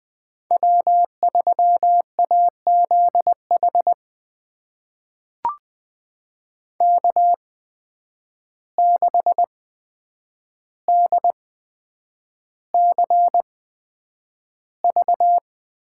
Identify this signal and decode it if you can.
Morse code, heard as a keyed tone, about 20 wpm, spaced as for 9 wpm, 700 Hz. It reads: W3AZH K6DCV
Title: USA Call Signs - 5 Characters - 8x Character Spacing - 20wpm